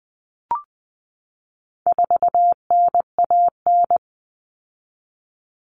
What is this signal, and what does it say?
Morse code, heard as a keyed tone, about 20 wpm, 700 Hz. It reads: 4NAN
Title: Common Cut Numbers - 4-Digit 20wpm